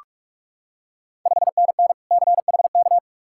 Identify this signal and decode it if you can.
Morse code, heard as a keyed tone, about 42 wpm, 700 Hz. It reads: 5NN CHK
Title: Ontario QSO Party Exchange 45wpm